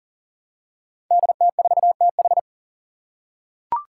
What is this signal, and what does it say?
Morse code, heard as a keyed tone, about 40 wpm, 700 Hz. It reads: DT4TH
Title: Five Challenging Letter-Number Combinations 40wpm